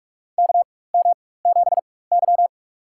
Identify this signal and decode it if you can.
Morse code, heard as a keyed tone, about 44 wpm, spaced as for 21 wpm, 700 Hz. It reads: KM7Y
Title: USA Call Signs - 4 Characters - 4x Character Spacing - 45wpm